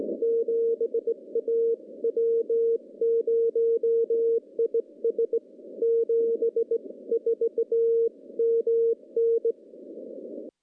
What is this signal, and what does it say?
Morse code, heard as a keyed tone, about 19 wpm, 460 Hz. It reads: 7 A W0IS 74 MN
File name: contests / CW-SS-k5tr- -0200